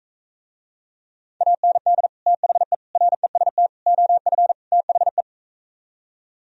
Morse code, heard as a keyed tone, about 45 wpm, 700 Hz. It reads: AND THE REST OF THE